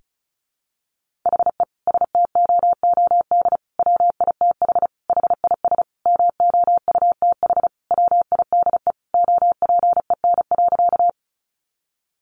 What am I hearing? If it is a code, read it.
Morse code, heard as a keyed tone, about 34 wpm, 700 Hz. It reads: HE STOOD WITH HIS MOUTH WIDE OPEN.
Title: Sentences from Top 1500 Words 35wpm